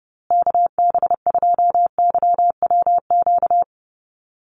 Morse code, heard as a keyed tone, about 30 wpm, 700 Hz. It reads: KB2YWQ
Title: USA Call Signs Repeated 2x 30wpm